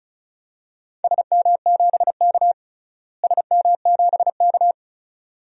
Morse code, heard as a keyed tone, about 36 wpm, 700 Hz. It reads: SM7K SM7K
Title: DXCC-100 Call Signs - 4 Characters - Repeated 2x - 35wpm